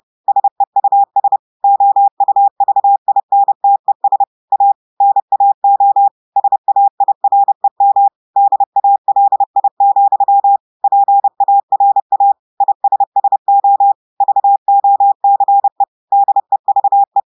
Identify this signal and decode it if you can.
Morse code, heard as a keyed tone, about 30 wpm, 800 Hz. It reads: SEUS OUVINTES A NAO SAIREM DALI, PARA ISSO VOCE DEVE
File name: audios / portugais / portugais texte en clair 2 vitesse 30 mots minutes